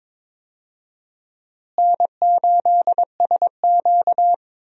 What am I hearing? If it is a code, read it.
Morse code, heard as a keyed tone, about 22 wpm, 700 Hz. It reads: N8SQ
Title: USA Call Signs - 4 Characters - Repeated 2x - 22wpm